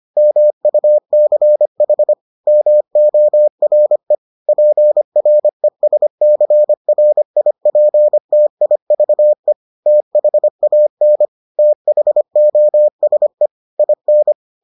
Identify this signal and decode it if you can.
Morse code, heard as a keyed tone, about 25 wpm, 600 Hz. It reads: MUCH MORE PRESCRIPTIVE THAN THOSE IN